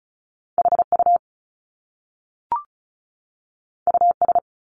Morse code, heard as a keyed tone, about 36 wpm, 700 Hz. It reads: HU US
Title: Top Two Letter Combinations in Words 35wpm